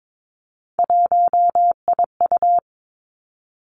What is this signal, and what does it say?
Morse code, heard as a keyed tone, about 22 wpm, 700 Hz. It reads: 1IU
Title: Three Challenging Letter-Number Combinations 22wpm